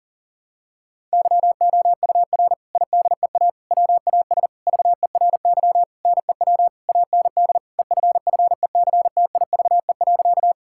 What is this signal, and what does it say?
Morse code, heard as a keyed tone, about 40 wpm, 700 Hz. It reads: YOUR IDEA WAS VERY NEW AND EFFECTIVE.
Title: Sentences from Top 1000 Words - Review Part 6 - 40wpm